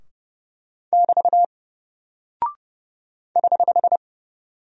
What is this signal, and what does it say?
Morse code, heard as a keyed tone, about 29 wpm, 700 Hz. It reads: <BT> <HH>
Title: Single Character (Letter:Number:Punctuation:Prosign) 30wpm